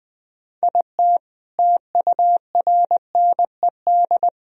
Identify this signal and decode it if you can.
Morse code, heard as a keyed tone, about 20 wpm, 700 Hz. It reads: IT TURNED
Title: Speed-Racing - Sets of 5 Words 17wpm